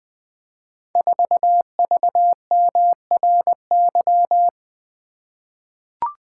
Morse code, heard as a keyed tone, about 20 wpm, 700 Hz. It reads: 4VMRY
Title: Code Groups - 5 Letter:Numbers - 20wpm